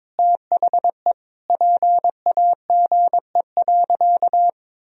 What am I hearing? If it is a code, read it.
Morse code, heard as a keyed tone, about 22 wpm, 700 Hz. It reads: THE PAGE.
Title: Sentences from Top 1000 Words - Review Part 1 - 22wpm